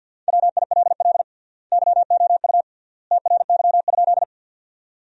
Morse code, heard as a keyed tone, about 50 wpm, 700 Hz. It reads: WILL YOU TRY?
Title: 3x Word-Spacing - Sentences from Top 200 Words - 50wpm